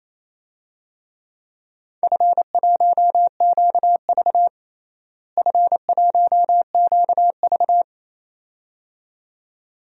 Morse code, heard as a keyed tone, about 28 wpm, 700 Hz. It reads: F1QV F1QV
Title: DXCC-100 Call Signs - 4 Characters - Repeated 2x - 28wpm